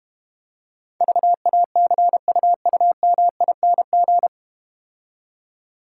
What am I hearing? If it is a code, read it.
Morse code, heard as a keyed tone, about 30 wpm, 700 Hz. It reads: VACUUMING
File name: Easy Words - Speed-Racing with Hints - 25wpm